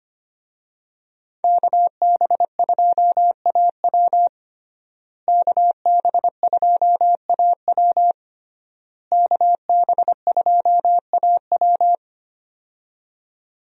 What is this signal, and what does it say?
Morse code, heard as a keyed tone, about 25 wpm, 700 Hz. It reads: KB2AW KB2AW KB2AW
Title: USA Call Signs - 5 Characters - Repeated 3x - 25wpm